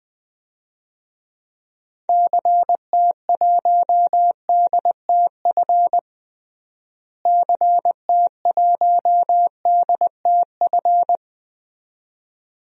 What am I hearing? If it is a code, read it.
Morse code, heard as a keyed tone, about 21 wpm, 700 Hz. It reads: CT1DTF CT1DTF
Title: DXCC-100 Call Signs - 6 Characters - Repeated 2x - 20wpm